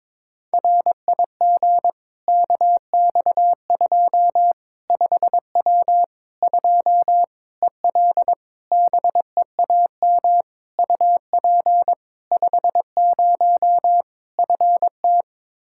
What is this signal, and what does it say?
Morse code, heard as a keyed tone, about 22 wpm, 700 Hz. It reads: RIG KX2 5W 2 EL BEAM UP 50 FT
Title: Practice QSO - Exchange 2 - Info - 22wpm